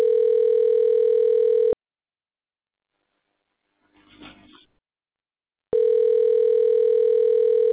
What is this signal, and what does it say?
Ringback tone (North America), 440 + 480 Hz.